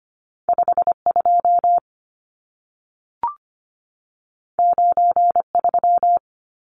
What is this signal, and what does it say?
Morse code, heard as a keyed tone, about 25 wpm, 700 Hz. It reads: 52 93
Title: Numbers - 2-Digit 25wpm